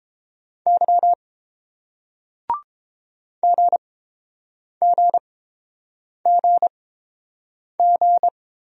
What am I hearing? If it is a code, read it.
Morse code, heard as a keyed tone, about 30 wpm, spaced as for 7 wpm, 700 Hz. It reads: Y GGGG